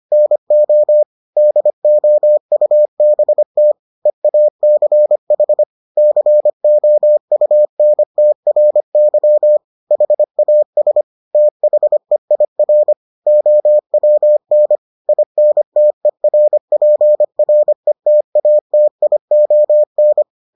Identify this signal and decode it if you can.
Morse code, heard as a keyed tone, about 25 wpm, 600 Hz. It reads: NO DOUBT EACH COUNTRY HAS THEIR OWN INTERPRETATION